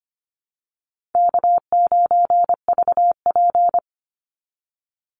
Morse code, heard as a keyed tone, about 24 wpm, 700 Hz. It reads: K9VP
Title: Call Signs - 25wpm